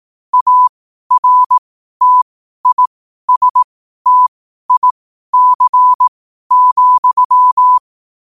Morse code, heard as a keyed tone, about 18 wpm, spaced as for 12 wpm, 1000 Hz. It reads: ARTISTIC,
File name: qotdfiles / 12wpm